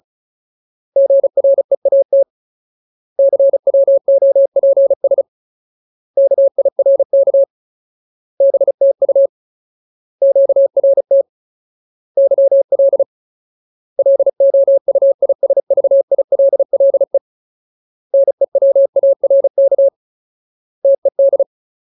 Morse code, heard as a keyed tone, about 36 wpm, 550 Hz. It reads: GREAT CWOPS KIRK BTU QRT YL LOUISVILLE NEWARK TED
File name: Ultimate Word List / 35WPM Ultimate